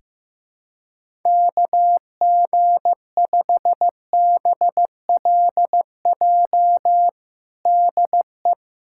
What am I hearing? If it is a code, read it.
Morse code, heard as a keyed tone, about 15 wpm, 700 Hz. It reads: KG5BLJ DE
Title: USA Callsign DE Callsign 15wpm